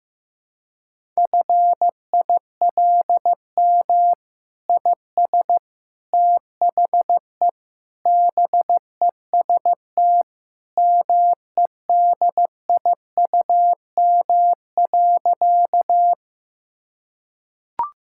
Morse code, heard as a keyed tone, about 16 wpm, 700 Hz. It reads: FILM IS THE BEST MEDIUM.
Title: Sentences from Top 1000 Words - Review Part 6 - 15wpm